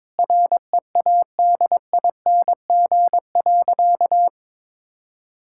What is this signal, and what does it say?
Morse code, heard as a keyed tone, about 21 wpm, 700 Hz. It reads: READING.
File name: Sentences with 7-Word Phrases - 22wpm